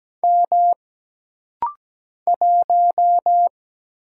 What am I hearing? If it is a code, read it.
Morse code, heard as a keyed tone, about 17 wpm, 700 Hz. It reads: M 1
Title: Single Letter-Number - Rapid-Fire 17wpm